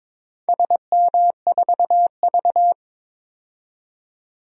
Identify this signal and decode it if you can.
Morse code, heard as a keyed tone, about 22 wpm, 700 Hz. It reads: SM4V
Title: DXCC-100 Call Signs - 4 Characters - 22wpm